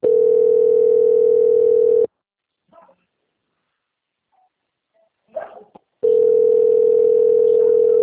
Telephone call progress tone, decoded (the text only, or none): ringback tone